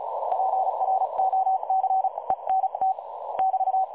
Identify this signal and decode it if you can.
Morse code, heard as a keyed tone, about 33 wpm, spaced as for 29 wpm, 760 Hz. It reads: PROCENT <BT>
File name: dlazaawansowanych / nagraniazpasma / websdr recording 3565.0kHz